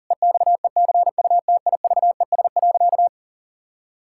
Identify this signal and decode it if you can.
Morse code, heard as a keyed tone, about 40 wpm, 700 Hz. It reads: EXECUTIVES.
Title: Pangram Challenge v2 - 40wpm